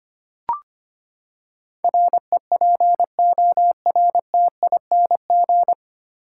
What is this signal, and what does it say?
Morse code, heard as a keyed tone, about 25 wpm, 700 Hz. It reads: REPORTING
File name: Nine Letter Words - 25wpm